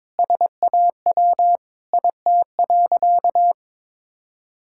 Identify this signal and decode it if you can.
Morse code, heard as a keyed tone, about 22 wpm, 700 Hz. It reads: SAW IT.
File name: practice / 22wpm / Sentences with 3-Word Phrases - 22wpm